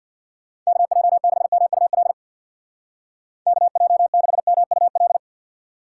Morse code, heard as a keyed tone, about 64 wpm, 700 Hz. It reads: KJ6MUL KJ6MUL